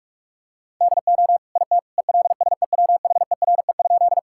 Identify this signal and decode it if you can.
Morse code, heard as a keyed tone, about 41 wpm, 700 Hz. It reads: DO IT ELSEWHERE?
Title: Sentences from Top 1600 Words 45wpm